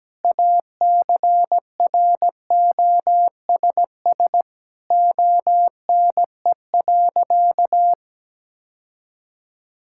Morse code, heard as a keyed tone, about 17 wpm, 700 Hz. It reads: ACROSS ONE.
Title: Sentences from Top 1000 Words - Review Part 5 - 17wpm